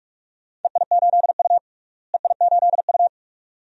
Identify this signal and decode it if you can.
Morse code, heard as a keyed tone, about 47 wpm, 700 Hz. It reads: EI8U EI8U
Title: DXCC-100 Call Signs - 4 Characters - Repeated 2x - 45wpm